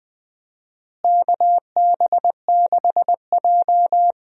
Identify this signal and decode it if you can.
Morse code, heard as a keyed tone, about 20 wpm, 700 Hz. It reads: KB6J
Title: USA Call Signs - 4 Characters - Repeated 2x - 20wpm